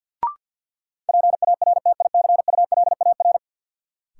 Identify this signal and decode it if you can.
Morse code, heard as a keyed tone, about 51 wpm, 700 Hz. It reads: PARTICULAR
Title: Sentences from Top 600 Words - New Word Drill - 50wpm